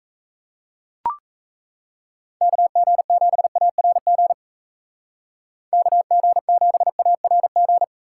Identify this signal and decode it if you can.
Morse code, heard as a keyed tone, about 34 wpm, 700 Hz. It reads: KG7ARG KG7ARG
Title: Speed Racing - USA 2x3 Call Signs 28wpm